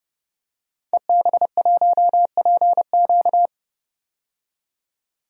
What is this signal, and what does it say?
Morse code, heard as a keyed tone, about 30 wpm, 700 Hz. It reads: EB1PQ
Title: DXCC-100 Call Signs - 5 Characters - 30wpm